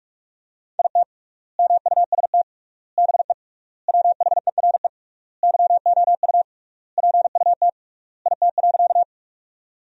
Morse code, heard as a keyed tone, about 46 wpm, 700 Hz. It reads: IT MUST BE WHERE YOU PUT IT.
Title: 3x Word-Spacing - Sentences from Top 200 Words - 45wpm